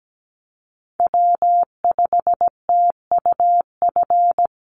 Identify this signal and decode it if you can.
Morse code, heard as a keyed tone, about 17 wpm, 700 Hz. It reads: W5TUF
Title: Call Signs Repeated 3x - 17wpm